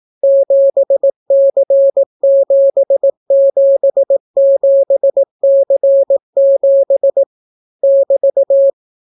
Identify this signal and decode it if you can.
Morse code, heard as a keyed tone, about 18 wpm, 550 Hz. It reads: 7C777C7 <BT>